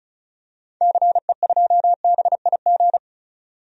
Morse code, heard as a keyed tone, about 36 wpm, 700 Hz. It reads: CE2BIG
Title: DXCC-100 Call Signs - 6 Characters - 35wpm